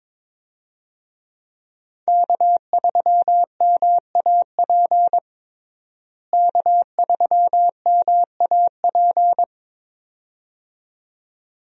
Morse code, heard as a keyed tone, about 22 wpm, 700 Hz. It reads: K3MAP K3MAP
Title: USA Call Signs - 5 Characters - Repeated 2x - 22wpm